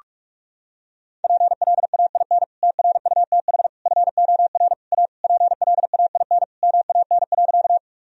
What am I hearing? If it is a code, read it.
Morse code, heard as a keyed tone, about 46 wpm, 700 Hz. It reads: PLAIN TRUTH FOR A PLAIN MAN.
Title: Sentences from Top 1000 Words - Review Part 2 - 45wpm